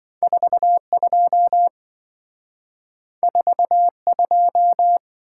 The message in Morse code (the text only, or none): 42 42